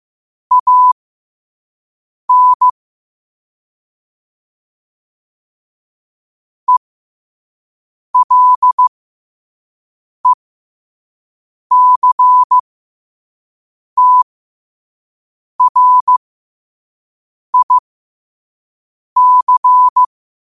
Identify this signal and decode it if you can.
Morse code, heard as a keyed tone, about 15 wpm, spaced as for 5 wpm, 1000 Hz. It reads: AN ELECTRIC
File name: qotdfiles / 5wpm